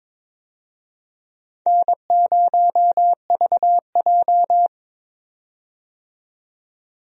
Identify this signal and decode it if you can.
Morse code, heard as a keyed tone, about 22 wpm, 700 Hz. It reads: N0VJ
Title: USA Call Signs - 4 Characters - 22wpm